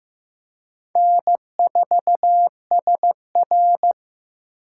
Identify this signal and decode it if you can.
Morse code, heard as a keyed tone, about 15 wpm, 700 Hz. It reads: N4SR
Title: K1USN SST QSO Practice - 15wpm